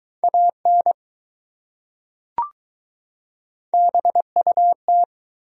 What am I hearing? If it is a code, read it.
Morse code, heard as a keyed tone, about 21 wpm, 700 Hz. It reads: AN BUT